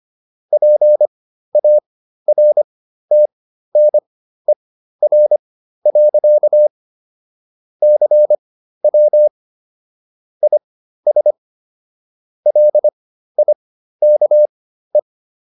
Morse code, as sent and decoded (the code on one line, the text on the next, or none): .--. .- .-. - -. .-. .-.-.- / -.-. .-- / .. ... / .-.. .. -.- .
PARTNER. CW IS LIKE